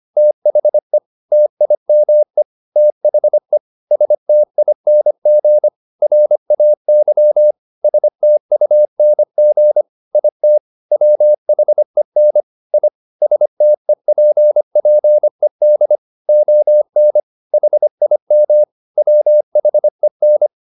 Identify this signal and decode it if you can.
Morse code, heard as a keyed tone, about 25 wpm, 600 Hz. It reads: THE TIME THE STING RAY STUNG IT WHEN I STEPPED ON HIM WHEN